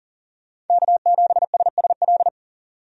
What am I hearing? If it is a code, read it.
Morse code, heard as a keyed tone, about 38 wpm, 700 Hz. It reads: K7SSL
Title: USA Call Signs - 5 Characters - 40wpm